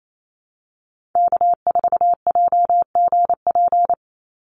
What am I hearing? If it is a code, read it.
Morse code, heard as a keyed tone, about 28 wpm, 700 Hz. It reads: K4JGP
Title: USA Call Signs 28wpm